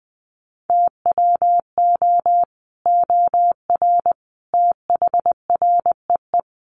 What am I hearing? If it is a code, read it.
Morse code, heard as a keyed tone, about 19 wpm, 700 Hz. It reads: TWO OR THREE
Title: Sets of 5 Words Encore2 20wpm